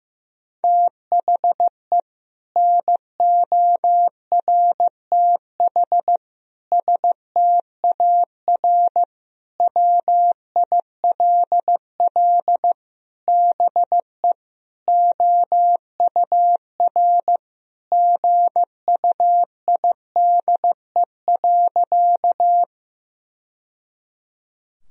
Morse code, heard as a keyed tone, about 15 wpm, 700 Hz. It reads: THE NORTH STAR WILL BE OUR GUIDE.
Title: Sentences from Top 1000 Words - Review Part 3 - 15wpm